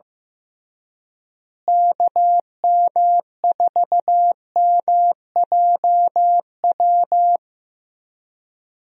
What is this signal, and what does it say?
Morse code, heard as a keyed tone, about 15 wpm, 700 Hz. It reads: KM4MJW
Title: USA Call Signs - 6 Characters - Repeated 3x - 15wpm